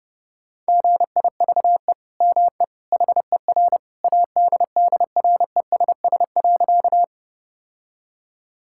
Morse code, heard as a keyed tone, about 30 wpm, 700 Hz. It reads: GIVE ME HER ADDRESS.
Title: Sentences from Top 1000 Words - Review Part 6 - 30wpm